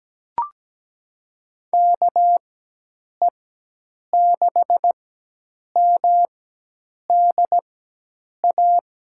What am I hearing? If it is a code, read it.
Morse code, heard as a keyed tone, about 17 wpm, spaced as for 8 wpm, 700 Hz. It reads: KE6MDA
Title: USA Call Signs - 4x Character Spacing - 17wpm